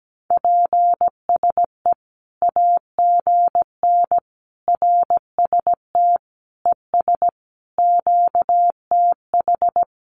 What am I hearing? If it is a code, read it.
Morse code, heard as a keyed tone, about 17 wpm, 700 Hz. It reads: PSE AGN RST ES QTH
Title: Common QSO Phrases 17wpm